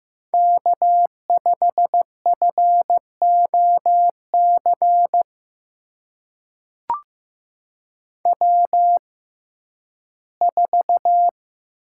Morse code, heard as a keyed tone, about 15 wpm, spaced as for 11 wpm, 700 Hz. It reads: K5FOC W4
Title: USA Call Signs - 5 Characters - 6x Character Spacing - 15wpm